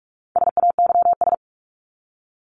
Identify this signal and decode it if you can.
Morse code, heard as a keyed tone, about 44 wpm, 700 Hz. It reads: SAYS
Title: ES:S Suffix - Top 100 Words with Base Words 45wpm